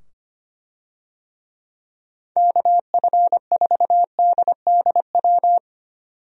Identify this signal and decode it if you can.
Morse code, heard as a keyed tone, about 25 wpm, 700 Hz. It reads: KF4DDW
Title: USA Call Signs - 6 Characters - 25wpm